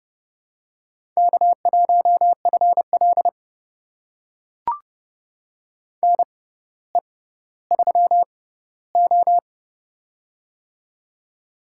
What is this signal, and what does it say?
Morse code, heard as a keyed tone, about 29 wpm, spaced as for 15 wpm, 700 Hz. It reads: K1FL NE3O
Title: USA Call Signs - 4 Characters - 6x Character Spacing - 30wpm